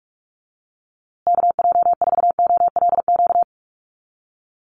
Morse code, heard as a keyed tone, about 44 wpm, 700 Hz. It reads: KJ4OLQ
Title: USA Call Signs Repeated 2x 45wpm